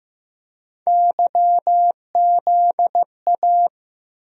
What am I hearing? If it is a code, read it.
Morse code, heard as a keyed tone, about 15 wpm, 700 Hz. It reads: YZA
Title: USA Call Sign Postfixes - 3 Letters 15wpm